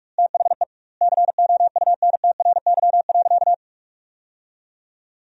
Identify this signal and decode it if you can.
Morse code, heard as a keyed tone, about 41 wpm, 700 Hz. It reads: THE COUNTRY.
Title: 2x Word-Spacing - Sentences from Top 300 Words - 45wpm